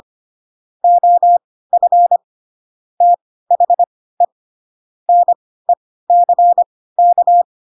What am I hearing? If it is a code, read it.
Morse code, heard as a keyed tone, about 25 wpm, spaced as for 16 wpm, 700 Hz. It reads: OF THE NECK
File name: Pinocchio / Pinocchio ch3 10 25 ik1hge